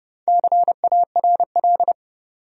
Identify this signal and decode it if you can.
Morse code, heard as a keyed tone, about 30 wpm, 700 Hz. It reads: CARL
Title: USA Common Names - 30wpm